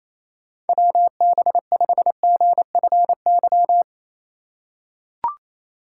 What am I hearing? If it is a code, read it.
Morse code, heard as a keyed tone, about 28 wpm, 700 Hz. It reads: WB5GFY
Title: USA Call Signs - 3x Character Spacing - 28wpm